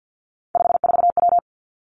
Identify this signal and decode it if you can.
Morse code, heard as a keyed tone, about 50 wpm, 700 Hz. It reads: 54W
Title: Three Challenging Letter-Number Combinations 50wpm